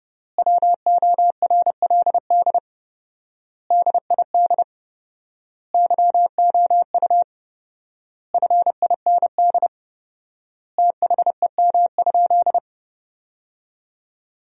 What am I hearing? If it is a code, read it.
Morse code, heard as a keyed tone, about 30 wpm, 700 Hz. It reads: WORLD DID YOU FIND THEM?
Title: 4x Word-Spacing - Sentences from Top 200 Words - 30wpm